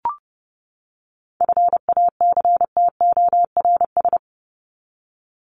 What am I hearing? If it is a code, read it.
Morse code, heard as a keyed tone, about 30 wpm, 700 Hz. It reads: FACTORS
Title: ES:S Suffix - Top 100 Words with Base Words 30wpm